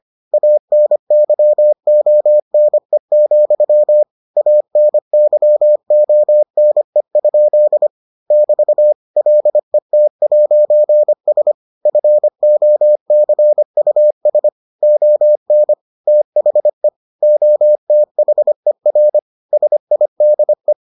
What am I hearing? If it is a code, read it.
Morse code, heard as a keyed tone, about 25 wpm, 600 Hz. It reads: ANYONE, ANYONE? <BT> LET'S FOCUS ON THE OTHER SIDE